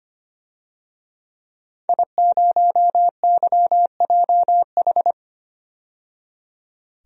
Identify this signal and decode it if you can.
Morse code, heard as a keyed tone, about 25 wpm, 700 Hz. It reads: I0YJH
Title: DXCC-100 Call Signs - 5 Characters - 25wpm